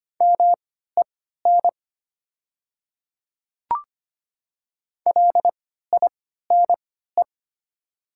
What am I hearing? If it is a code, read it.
Morse code, heard as a keyed tone, about 25 wpm, spaced as for 14 wpm, 700 Hz. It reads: MEN LINE
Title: 3x Character Spacing - Top 200 Words - 25wpm